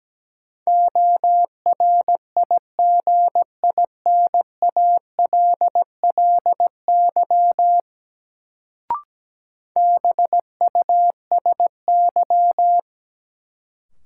Morse code, heard as a keyed tone, about 17 wpm, 700 Hz. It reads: ORIGINALLY BUSY
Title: Sentences from Top 1500 Words - New Word Drill Alt - 17wpm